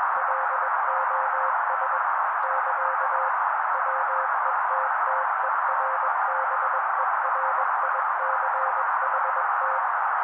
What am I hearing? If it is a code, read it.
Morse code, heard as a keyed tone, about 21 wpm, 530 Hz. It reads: LOS <KA> WETTERBERICHT
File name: daily / 20wpm noise